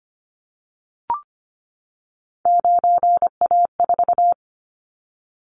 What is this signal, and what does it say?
Morse code, heard as a keyed tone, about 25 wpm, 700 Hz. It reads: 9A4
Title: Callsign Prefixes for Top 50 DXCC - 25wpm